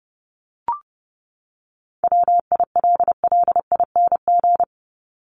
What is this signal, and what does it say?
Morse code, heard as a keyed tone, about 30 wpm, 700 Hz. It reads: WILLING
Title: Top 751-1000 Words 30wpm